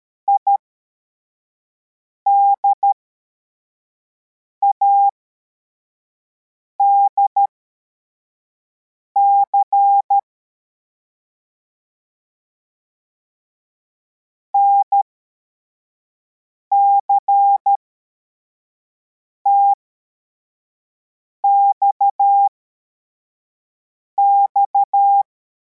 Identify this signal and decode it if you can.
Morse code, heard as a keyed tone, about 13 wpm, spaced as for 4 wpm, 800 Hz. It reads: IDADC NCTXX
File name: formation / cw / Text33